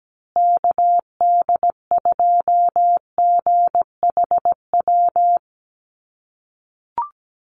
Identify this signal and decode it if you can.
Morse code, heard as a keyed tone, about 17 wpm, 700 Hz. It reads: KD2GHW
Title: USA Call Signs Repeated 3x 17wpm